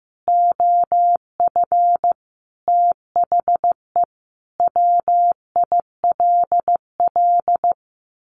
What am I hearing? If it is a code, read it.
Morse code, heard as a keyed tone, about 15 wpm, 700 Hz. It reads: OF THE WILL
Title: Sentences from Top 1500 Words 15wpm